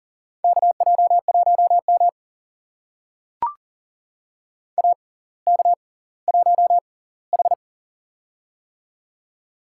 Morse code, heard as a keyed tone, about 40 wpm, spaced as for 20 wpm, 700 Hz. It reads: KJ1M AK1H
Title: USA Call Signs - 4 Characters - 6x Character Spacing - 40wpm